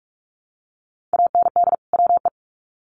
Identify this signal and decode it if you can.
Morse code, heard as a keyed tone, about 44 wpm, 700 Hz. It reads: AND WE
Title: Sets of 2 Words 45wpm